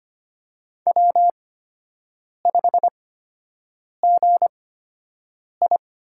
Morse code, heard as a keyed tone, about 25 wpm, spaced as for 7 wpm, 700 Hz. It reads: W5GI
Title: USA Call Signs - 4 Characters - 8x Character Spacing - 25wpm